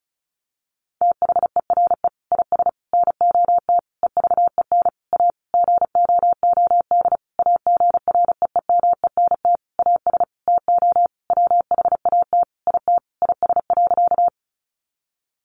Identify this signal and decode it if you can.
Morse code, heard as a keyed tone, about 34 wpm, 700 Hz. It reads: THERE IS NOT EVEN A GOOD AGREEMENT AS TO WHAT IT IS.
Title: Sentences from Top 1500 Words 35wpm